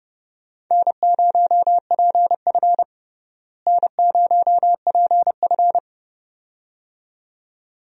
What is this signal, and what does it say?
Morse code, heard as a keyed tone, about 32 wpm, 700 Hz. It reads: N0PF N0PF
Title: USA Call Signs - 4 Characters - Repeated 2x - 30wpm